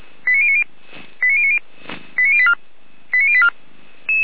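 Five-tone selective call (ZVEI), heard as quick runs of five tones, tones of about 70 ms. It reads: 89090 89090 89084 89084
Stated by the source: ZVEI five-tone call received on an SDR.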